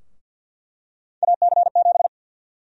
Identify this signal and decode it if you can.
Morse code, heard as a keyed tone, about 46 wpm, 700 Hz. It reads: AC7
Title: USA Call Sign Prefixes - Repeated 2x - 50wpm